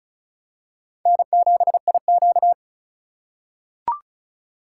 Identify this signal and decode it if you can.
Morse code, heard as a keyed tone, about 36 wpm, 700 Hz. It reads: N7IQ
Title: USA Call Signs - 4 Characters - 4x Character Spacing - 35wpm